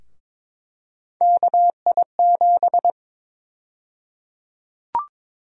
Morse code, heard as a keyed tone, about 22 wpm, 700 Hz. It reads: KI7
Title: USA Call Sign Prefixes - 22wpm